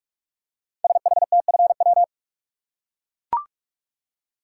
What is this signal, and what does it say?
Morse code, heard as a keyed tone, about 44 wpm, 700 Hz. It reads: SHTFW
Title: Code Groups - 5 Letters - 45wpm